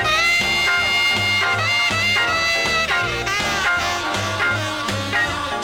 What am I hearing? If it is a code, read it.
Dial-up modem handshake.